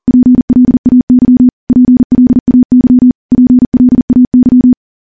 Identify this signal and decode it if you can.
Morse code, heard as a keyed tone, about 42 wpm, 250 Hz. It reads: PLAY PLAY PLAY
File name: morse / morse-training-40WPM